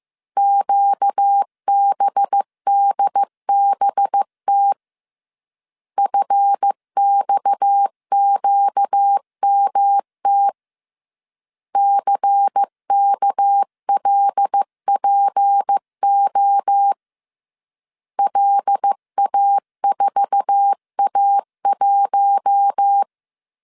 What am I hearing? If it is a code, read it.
Morse code, heard as a keyed tone, about 15 wpm, 800 Hz. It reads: QBDBT FXQMT CKLPO LA4A1
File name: CW / DET / 1 9 1500C S A 14wpm 3222